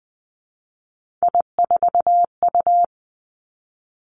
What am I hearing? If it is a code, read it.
Morse code, heard as a keyed tone, about 20 wpm, 700 Hz. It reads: I4U
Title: Three Challenging Letter-Number Combinations 20wpm